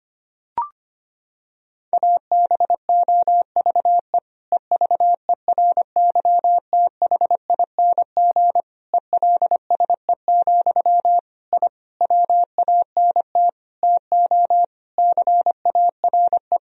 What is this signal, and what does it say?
Morse code, heard as a keyed tone, about 25 wpm, 700 Hz. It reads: ABOVE EVERYTHING ELSE, I WANT TO CARE
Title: Sentences from Top 1000 Words - Review Part 2 - 25wpm